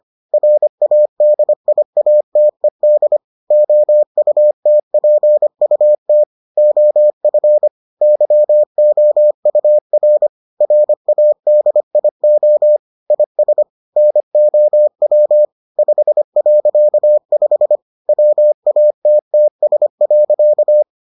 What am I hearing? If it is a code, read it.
Morse code, heard as a keyed tone, about 25 wpm, 600 Hz. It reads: RADIATED OUTPUT OF YOUR RADIO IS NOW 5.5 WATTS.